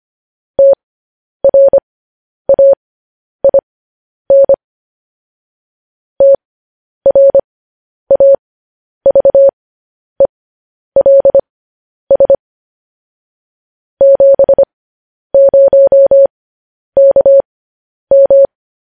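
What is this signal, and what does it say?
Morse code, heard as a keyed tone, about 25 wpm, spaced as for 10 wpm, 550 Hz. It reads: TRAIN TRAVELS 70KM
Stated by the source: Morse code news headlines practice, at 10 WPM.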